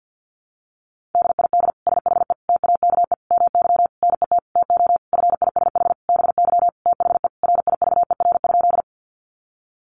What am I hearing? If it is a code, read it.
Morse code, heard as a keyed tone, about 50 wpm, 700 Hz. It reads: DID SHE TAKE MY NET TO FISH BY THE RIVER?
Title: Sentences from Top 1400 Words 50wpm